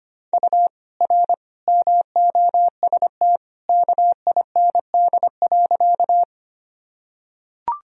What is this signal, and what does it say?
Morse code, heard as a keyed tone, about 25 wpm, 700 Hz. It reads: U R MOST KIND.
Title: Sentences from Top 200 Words and Abbreviations 25wpm